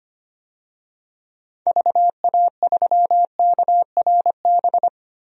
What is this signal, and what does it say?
Morse code, heard as a keyed tone, about 25 wpm, 700 Hz. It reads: VA3KRB
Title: DXCC-100 Call Signs - 6 Characters - 25wpm